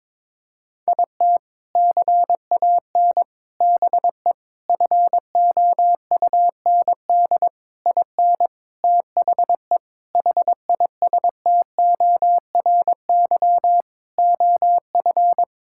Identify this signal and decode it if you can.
Morse code, heard as a keyed tone, about 22 wpm, 700 Hz. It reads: IT CAN BE FOUND IN THE HISTORY OF
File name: practice / 22wpm / Sentences with 4-Word Phrases - 22wpm